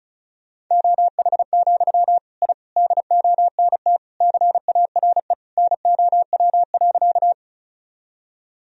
Morse code, heard as a keyed tone, about 34 wpm, 700 Hz. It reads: OH, I DONT CARE NOW.
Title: Sentences from Top 1000 Words - Review Part 1 - 35wpm